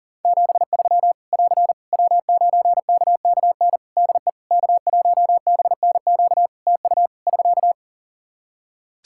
Morse code, heard as a keyed tone, about 40 wpm, 700 Hz. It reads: 73 <AR> W9KKN DE K1BNQ TU <SK>
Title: K1USN SST QSO Practice 40wpm